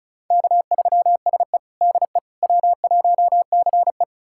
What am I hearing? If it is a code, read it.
Morse code, heard as a keyed tone, about 36 wpm, 700 Hz. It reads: K3SE DE W1CE
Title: USA Callsign DE Callsign - 4 Characters - 35wpm